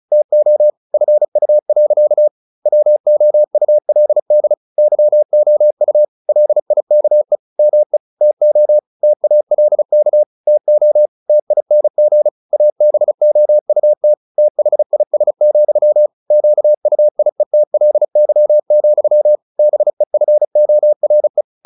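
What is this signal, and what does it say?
Morse code, heard as a keyed tone, about 34 wpm, 600 Hz. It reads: TO FU. WOULD YOU LIKE ME TO TALK TO TING ABOUT THIS, QUIETLY, BEFORE
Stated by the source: Morse practice at 35 wpm from CW med Gnister, sending Ken Follett's novel Never in English.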